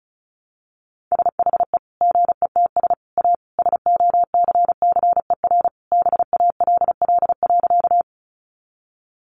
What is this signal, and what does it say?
Morse code, heard as a keyed tone, about 34 wpm, 700 Hz. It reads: SHE GETS A SOCCER BALL.
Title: Easy Phrases 35wpm